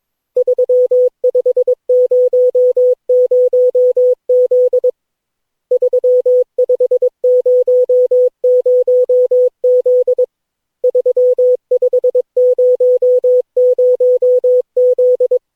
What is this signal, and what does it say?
Morse code, heard as a keyed tone, about 22 wpm, 500 Hz. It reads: 3500Z 3500Z 3500Z
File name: cw / Session9 cw training